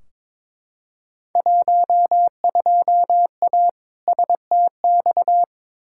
Morse code, heard as a keyed tone, about 22 wpm, 700 Hz. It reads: 12A STX
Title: Field Day Exchanges Encore - 22wpm